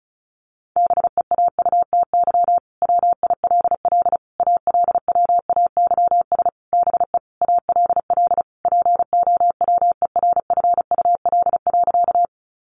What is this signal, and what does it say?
Morse code, heard as a keyed tone, about 34 wpm, 700 Hz. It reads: BEAUTY WILL ALWAYS BE ALL POWERFUL.
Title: Sentences from Top 1300 Words 35wpm